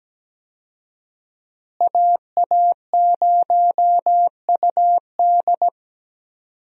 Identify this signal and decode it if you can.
Morse code, heard as a keyed tone, about 17 wpm, 700 Hz. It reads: AA0UD
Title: USA Call Signs - 5 Characters - Repeated 3x - 17wpm